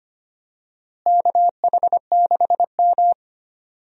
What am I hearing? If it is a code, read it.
Morse code, heard as a keyed tone, about 25 wpm, 700 Hz. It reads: KH6M
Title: DXCC-100 Call Signs - 4 Characters - Repeated 2x - 25wpm